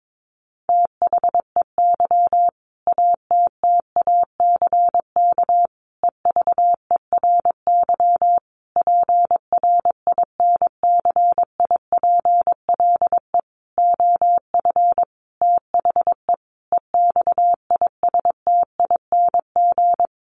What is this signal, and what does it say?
Morse code, heard as a keyed tone, about 22 wpm, 700 Hz. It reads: THEY ATTACK EVERY PRINCIPLE OF THE EXISTING
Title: Sentences from Top 1500 Words 22wpm